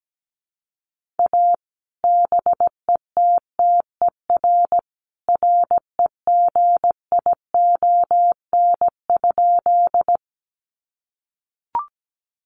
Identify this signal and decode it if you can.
Morse code, heard as a keyed tone, about 17 wpm, 700 Hz. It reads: A BETTER REGION?
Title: Sentences from Top 1500 Words 17wpm